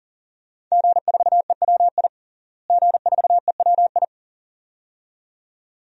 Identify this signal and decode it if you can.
Morse code, heard as a keyed tone, about 40 wpm, 700 Hz. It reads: G4EWI G4EWI
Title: DXCC-100 Call Signs - 5 Characters - Repeated 2x - 40wpm